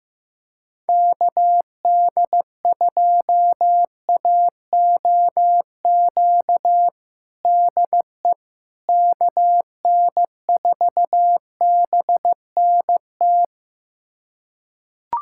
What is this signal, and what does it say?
Morse code, heard as a keyed tone, about 15 wpm, 700 Hz. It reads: KD2AOQ DE KN4BNT
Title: USA Callsign DE Callsign - 6 Characters - 15wpm